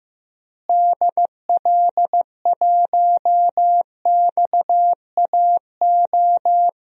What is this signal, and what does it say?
Morse code, heard as a keyed tone, about 15 wpm, 700 Hz. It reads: DL1XAO
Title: DXCC-100 Call Signs - 6 Characters - Repeated 2x - 15wpm